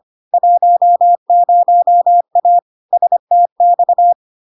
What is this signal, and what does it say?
Morse code, heard as a keyed tone, about 25 wpm, 700 Hz. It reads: 10A STX
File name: LARK-FD-2022-Log-in-CW-25wpm-700-Hz